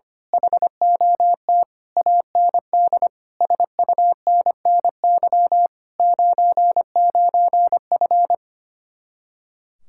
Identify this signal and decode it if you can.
Morse code, heard as a keyed tone, about 24 wpm, 700 Hz. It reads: HOT AND SUNNY 99F
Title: Speed-Racing Common QSO Phrases Encore - 25wpm